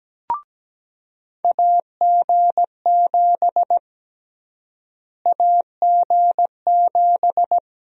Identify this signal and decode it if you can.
Morse code, heard as a keyed tone, about 17 wpm, 700 Hz. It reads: AG7 AG7
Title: USA Call Sign Prefixes - Repeated 2x - 17wpm